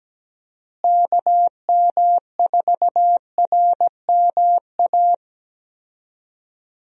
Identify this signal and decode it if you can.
Morse code, heard as a keyed tone, about 17 wpm, 700 Hz. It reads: KM4RMA
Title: USA Call Signs - 6 Characters - Repeated 2x - 17wpm